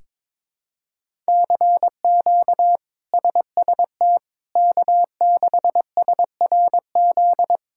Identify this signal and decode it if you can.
Morse code, heard as a keyed tone, about 21 wpm, 700 Hz. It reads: CQ SST K6SRZ
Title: K1USN SST QSO Practice - 22wpm